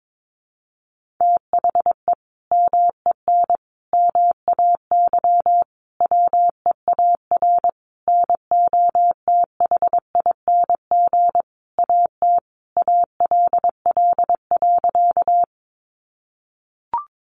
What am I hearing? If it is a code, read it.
Morse code, heard as a keyed tone, about 22 wpm, 700 Hz. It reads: THE MEN MAY WEAR NOTHING AT ALL.
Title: Sentences from Top 1500 Words 22wpm